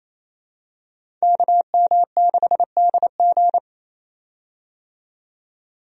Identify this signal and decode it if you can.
Morse code, heard as a keyed tone, about 28 wpm, 700 Hz. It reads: KM6DG
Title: USA Call Signs - 5 Characters - 28wpm